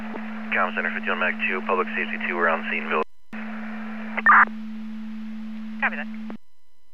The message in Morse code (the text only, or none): none